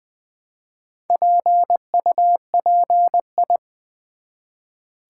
Morse code, heard as a keyed tone, about 20 wpm, 700 Hz. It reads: PUPI
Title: Build A Word - 1 Letter At A Time - 3 to 5-Letter Words - 20wpm